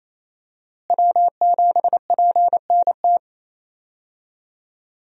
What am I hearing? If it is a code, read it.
Morse code, heard as a keyed tone, about 28 wpm, 700 Hz. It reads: W7PNT
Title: USA Call Signs - 5 Characters - Repeated 2x - 28wpm